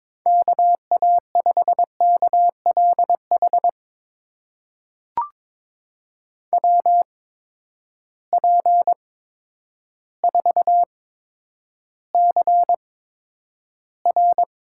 Morse code, heard as a keyed tone, about 22 wpm, spaced as for 10 wpm, 700 Hz. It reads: KA5KLH WP4CR
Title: USA Call Signs - 6 Characters - 8x Character Spacing - 22wpm